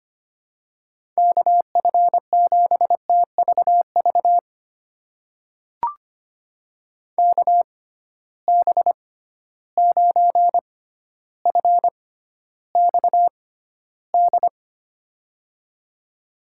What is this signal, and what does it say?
Morse code, heard as a keyed tone, about 25 wpm, spaced as for 13 wpm, 700 Hz. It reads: KF7TVV KB9FXD
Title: USA Call Signs - 6 Characters - 6x Character Spacing - 25wpm